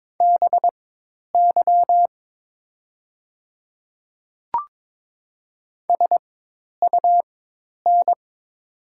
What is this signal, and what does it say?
Morse code, heard as a keyed tone, about 22 wpm, spaced as for 10 wpm, 700 Hz. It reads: BY SUN
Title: 4x Character Spacing - Top 300 Words - 22wpm